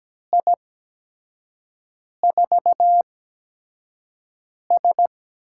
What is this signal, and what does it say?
Morse code, heard as a keyed tone, about 17 wpm, spaced as for 5 wpm, 700 Hz. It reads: I4S
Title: USA Call Signs - 4 Characters - 8x Character Spacing - 17wpm